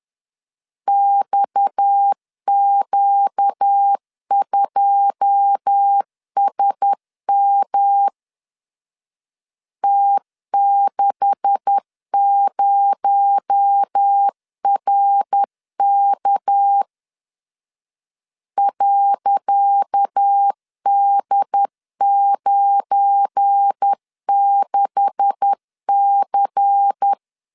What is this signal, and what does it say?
Morse code, heard as a keyed tone, about 10 wpm, 800 Hz. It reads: XQ2SM T60RK .D96C
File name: LEZIONI / 10 SIGNS / 10C DOT 10wpm 2940